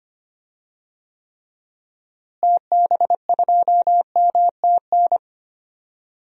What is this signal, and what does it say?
Morse code, heard as a keyed tone, about 25 wpm, 700 Hz. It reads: TB2MTN
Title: DXCC-100 Call Signs - 6 Characters - Repeated 2x - 25wpm